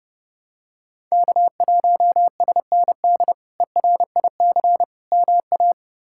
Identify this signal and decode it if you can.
Morse code, heard as a keyed tone, about 32 wpm, 700 Hz. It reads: K1SND ERIC MA
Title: K1USN SST QSO Practice - 30wpm